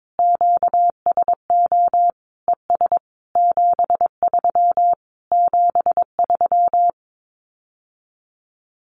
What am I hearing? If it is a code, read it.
Morse code, heard as a keyed tone, about 22 wpm, 700 Hz. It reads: QSO ES 73 73
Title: Common QSO Phrases Encore 22wpm